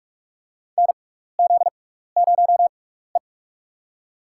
Morse code, heard as a keyed tone, about 44 wpm, spaced as for 15 wpm, 700 Hz. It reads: NZ0E
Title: USA Call Signs - 4 Characters - 6x Character Spacing - 45wpm